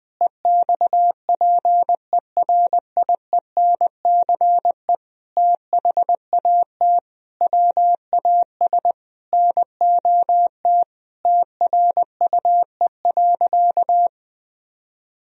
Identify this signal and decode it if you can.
Morse code, heard as a keyed tone, about 20 wpm, 700 Hz. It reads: EXPERIENCE THAT WAS NOT TRUE.
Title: Sentences from Top 1000 Words - Review Part 4 - 20wpm